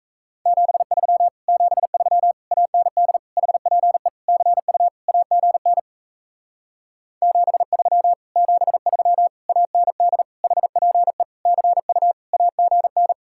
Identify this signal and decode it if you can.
Morse code, heard as a keyed tone, about 37 wpm, 700 Hz. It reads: 73 73 AND HPE CU AGN 73 73 AND HPE CU AGN